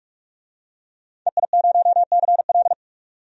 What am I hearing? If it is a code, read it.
Morse code, heard as a keyed tone, about 44 wpm, 700 Hz. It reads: EI0CL
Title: DXCC-100 Call Signs - 5 Characters - 45wpm